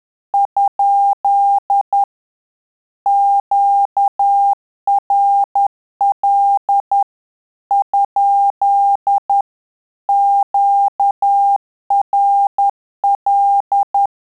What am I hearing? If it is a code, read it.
Morse code, heard as a keyed tone, about 10 wpm, 800 Hz. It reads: ? QRL ? QRL
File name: Telegraphy-group / QSO CW 10 PPM